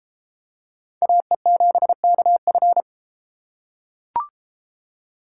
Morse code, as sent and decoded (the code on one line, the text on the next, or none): .- . --... -.- ..-.
AE7KF